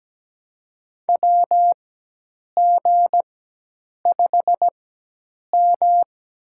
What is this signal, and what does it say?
Morse code, heard as a keyed tone, about 17 wpm, spaced as for 8 wpm, 700 Hz. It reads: WG5M
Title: USA Call Signs - 4 Characters - 4x Character Spacing - 17wpm